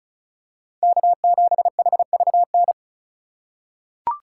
Morse code, heard as a keyed tone, about 34 wpm, 700 Hz. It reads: K7HVN
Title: USA Call Signs - 5 Characters - 3x Character Spacing - 35wpm